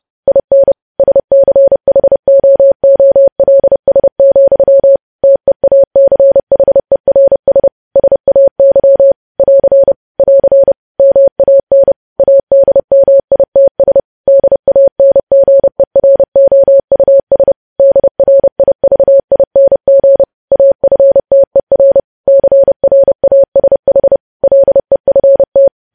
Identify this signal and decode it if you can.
Morse code, heard as a keyed tone, about 30 wpm, 550 Hz. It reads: IN SCHOOLS, TEACHERS SAY <AR> <AR> MAN ADMITS DANGEROUS DRIVING AFTER CRASH LEFT
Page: news headlines in morse code at 30 wpm